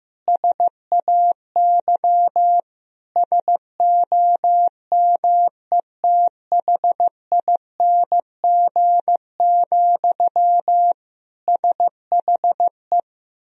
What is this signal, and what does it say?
Morse code, heard as a keyed tone, about 15 wpm, 700 Hz. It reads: SAY SOMETHING, SHE
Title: Sentences from Top 1000 Words - Review Part 3 - 15wpm